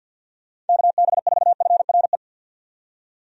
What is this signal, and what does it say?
Morse code, heard as a keyed tone, about 52 wpm, 700 Hz. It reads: KB4FRE
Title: USA Call Signs - 6 Characters - 50wpm